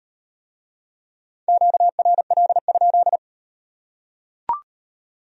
Morse code, heard as a keyed tone, about 38 wpm, 700 Hz. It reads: QRL?